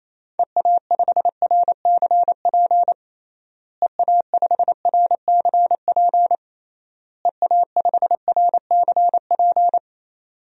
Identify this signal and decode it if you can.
Morse code, heard as a keyed tone, about 28 wpm, 700 Hz. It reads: EA5RCP EA5RCP EA5RCP
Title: DXCC-100 Call Signs - 6 Characters - Repeated 3x - 28wpm